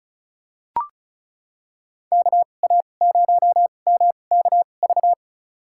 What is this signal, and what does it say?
Morse code, heard as a keyed tone, about 36 wpm, spaced as for 25 wpm, 700 Hz. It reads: KA0MKV
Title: USA Call Signs - 6 Characters - 2x Character Spacing - 35wpm